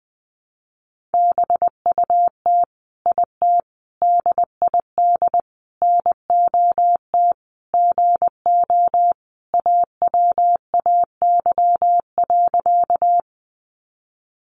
Morse code, heard as a keyed tone, about 20 wpm, 700 Hz. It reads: BUT IT DID NOT GO AWAY.
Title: Sentences from Top 600 Words 20wpm